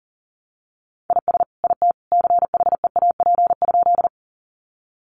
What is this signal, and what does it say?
Morse code, heard as a keyed tone, about 40 wpm, 700 Hz. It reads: IS IT CHEAP?
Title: Sentences from Top 1200 Words 40wpm